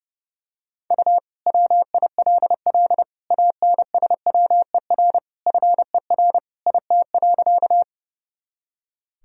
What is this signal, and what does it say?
Morse code, heard as a keyed tone, about 30 wpm, 700 Hz. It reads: U WILL ANSWER FER IT.
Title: Sentences from Top 300 Words and Abbreviations 30wpm